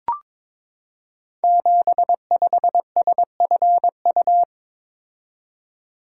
Morse code, heard as a keyed tone, about 22 wpm, 700 Hz. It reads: 75SFU
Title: Code Groups - 5 Letter:Numbers - 22wpm